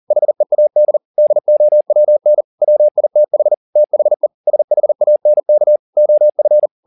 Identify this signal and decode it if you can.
Morse code, heard as a keyed tone, about 40 wpm, 600 Hz. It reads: HEAD DOWN WITH THE SHANK OF